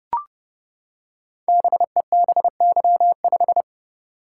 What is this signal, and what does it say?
Morse code, heard as a keyed tone, about 29 wpm, 700 Hz. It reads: BEBY5
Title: Code Groups - 5 Letter:Numbers - 30wpm